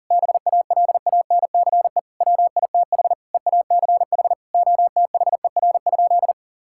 Morse code, heard as a keyed tone, about 40 wpm, 700 Hz. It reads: BALANCE WITH EACH OTHER?
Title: Sentences from Top 1000 Words - Review Part 6 - 40wpm